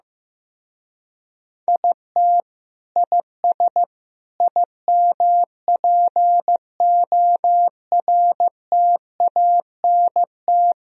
Morse code, heard as a keyed tone, about 15 wpm, 700 Hz. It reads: IT IS IMPORTANT
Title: Speed-Racing - Sets of 4 Words 15wpm